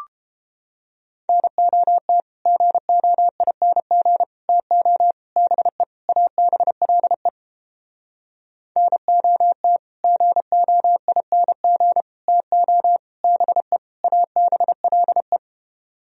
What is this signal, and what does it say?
Morse code, heard as a keyed tone, about 31 wpm, 700 Hz. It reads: NOT GOING TO BE ABLE NOT GOING TO BE ABLE